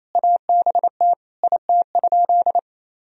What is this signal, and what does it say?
Morse code, heard as a keyed tone, about 28 wpm, 700 Hz. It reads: ABT IT?
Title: Sentences from Top 200 Words and Abbreviations 28wpm